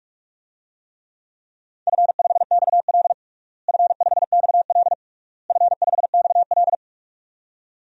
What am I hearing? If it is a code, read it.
Morse code, heard as a keyed tone, about 46 wpm, 700 Hz. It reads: F5XL F5XL F5XL
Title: DXCC-100 Call Signs - 4 Characters - Repeated 3x - 45wpm